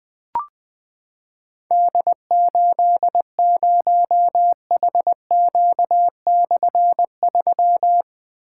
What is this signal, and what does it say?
Morse code, heard as a keyed tone, about 20 wpm, 700 Hz. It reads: D80HQ/3
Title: Difficult Call Signs 20wpm